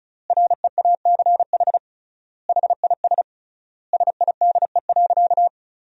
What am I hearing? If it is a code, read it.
Morse code, heard as a keyed tone, about 36 wpm, 700 Hz. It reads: REACH HIS SIDE.
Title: 3x Word-Spacing - Sentences from Top 400 Words - 35wpm